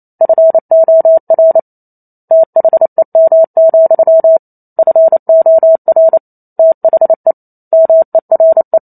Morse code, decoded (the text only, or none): FOR THEM, FOR THE MERE